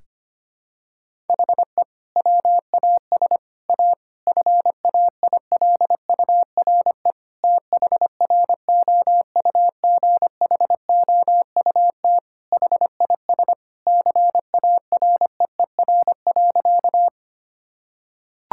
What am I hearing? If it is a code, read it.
Morse code, heard as a keyed tone, about 25 wpm, 700 Hz. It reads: HE WAS A FAILURE THROUGHOUT HIS CAREER.
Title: Sentences from Top 1000 Words - Review Part 6 - 25wpm